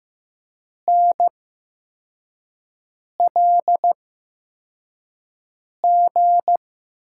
Morse code, heard as a keyed tone, about 15 wpm, spaced as for 4 wpm, 700 Hz. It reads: NLG
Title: USA Call Signs - 6 Characters - 8x Character Spacing - 15wpm